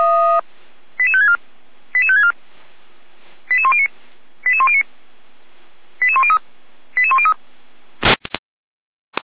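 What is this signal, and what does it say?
Five-tone selective call (ZVEI), heard as quick runs of five tones, tones of about 70 ms. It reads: 88564 88564 88109 88109 88103 88103
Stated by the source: SDR receiver recording of ZVEI five-tone selective call 88102, from feuerwehr-orlamuende.org fire brigade server.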